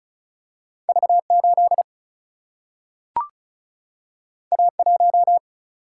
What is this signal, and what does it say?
Morse code, heard as a keyed tone, about 36 wpm, 700 Hz. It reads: V 8 A 1
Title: Single-Letter Single-Number 35wpm